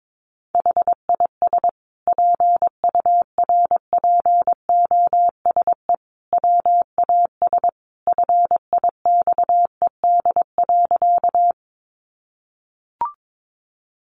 Morse code, heard as a keyed tone, about 22 wpm, 700 Hz. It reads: HIS PURPOSE WAS FIXED.
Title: Sentences from Top 1100 Words 22wpm